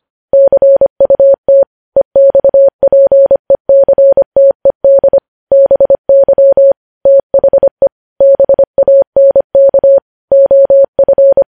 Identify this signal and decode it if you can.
Morse code, heard as a keyed tone, about 25 wpm, 550 Hz. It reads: CUT EXPECTED BY THE BANK OF
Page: news headlines in morse code at 25 wpm